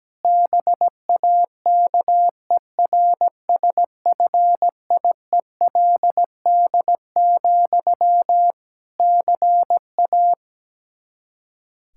Morse code, heard as a keyed tone, about 17 wpm, 700 Hz. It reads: BAKERSFIELD, CA
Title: K1USN SST QSO Practice 17wpm